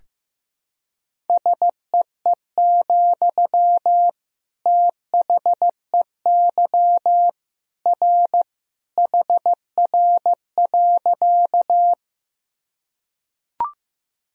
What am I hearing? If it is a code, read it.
Morse code, heard as a keyed tone, about 15 wpm, 700 Hz. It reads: SEE, THEY R HR.
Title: Sentences from Top 200 Words and Abbreviations 15wpm